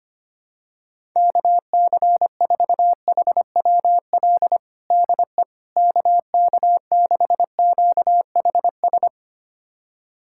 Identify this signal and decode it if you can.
Morse code, heard as a keyed tone, about 25 wpm, 700 Hz. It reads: KC4HWL DE KK6QHS
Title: USA Callsign DE Callsign 25wpm